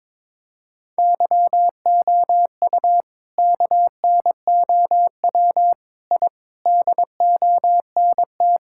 Morse code, heard as a keyed tone, about 22 wpm, 700 Hz. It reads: YOU KNOW I DONT
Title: Sentences from Top 1000 Words - Review Part 6 - 22wpm